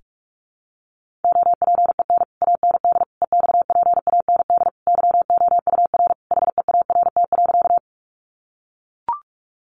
Morse code, heard as a keyed tone, about 44 wpm, 700 Hz. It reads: OPEN AND EXPAND YOUR HEART.
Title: Sentences from Top 1400 Words 45wpm